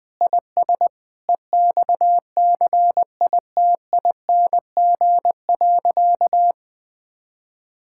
Morse code, heard as a keyed tone, about 20 wpm, 700 Hz. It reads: IS EXCITING.
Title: Sentences with 6-Word Phrases - 20wpm